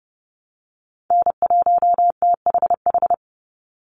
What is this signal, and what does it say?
Morse code, heard as a keyed tone, about 30 wpm, 700 Hz. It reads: N1THH
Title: USA Call Signs 30wpm